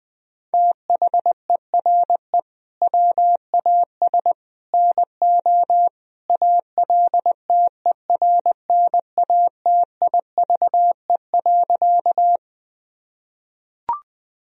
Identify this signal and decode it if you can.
Morse code, heard as a keyed tone, about 21 wpm, 700 Hz. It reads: THERE WAS NO ALTERNATIVE.
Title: Sentences from Top 1000 Words - Review Part 4 - 20wpm